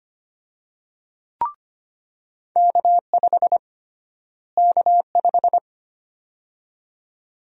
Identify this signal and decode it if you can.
Morse code, heard as a keyed tone, about 25 wpm, 700 Hz. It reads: K5 K5
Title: USA Call Sign Prefixes - Repeated 2x - 25wpm